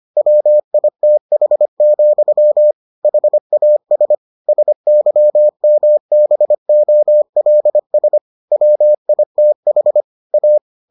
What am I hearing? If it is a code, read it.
Morse code, heard as a keyed tone, about 25 wpm, 600 Hz. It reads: WITH, HAS SYMBOLS WITH A